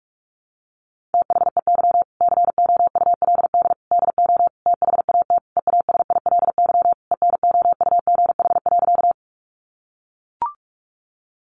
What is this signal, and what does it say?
Morse code, heard as a keyed tone, about 44 wpm, 700 Hz. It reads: THEY COULD DO THAT EASILY ENOUGH.
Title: Sentences from Top 900 Words 45wpm